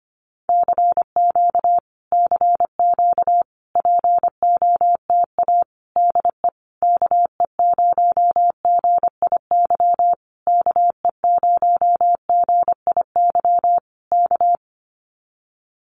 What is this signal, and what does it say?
Morse code, heard as a keyed tone, about 25 wpm, 700 Hz. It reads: CQ CQ POTA DE KE0GIY KE0GIY K
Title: Common QSO Phrases Encore 25wpm